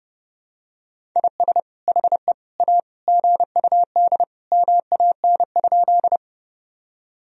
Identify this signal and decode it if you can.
Morse code, heard as a keyed tone, about 30 wpm, 700 Hz. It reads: IS HE A GUD MAN?
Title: Sentences from Top 200 Words and Abbreviations 30wpm